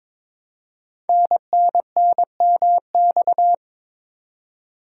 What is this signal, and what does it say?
Morse code, heard as a keyed tone, about 22 wpm, 700 Hz. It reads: NNNMX
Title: Code Groups - 5 Letters - 22wpm